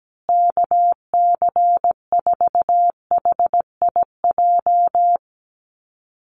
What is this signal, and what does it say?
Morse code, heard as a keyed tone, about 17 wpm, 700 Hz. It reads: KC4HIJ
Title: USA Call Signs Repeated 2x 17wpm